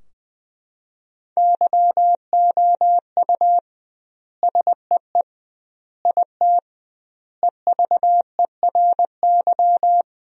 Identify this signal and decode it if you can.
Morse code, heard as a keyed tone, about 21 wpm, 700 Hz. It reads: YOU SEE IT EVERY
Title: 2x Word-Spacing - Sentences from Top 200 Words - 20wpm